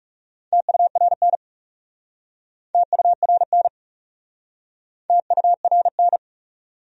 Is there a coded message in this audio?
Yes: Morse code from 0.5 to 6.2 s, heard as a keyed tone, about 40 wpm, 700 Hz. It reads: TURN TURN TURN